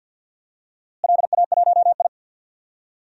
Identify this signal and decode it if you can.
Morse code, heard as a keyed tone, about 53 wpm, 700 Hz. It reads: LA1I
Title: DXCC-100 Call Signs - 4 Characters - Repeated 3x - 50wpm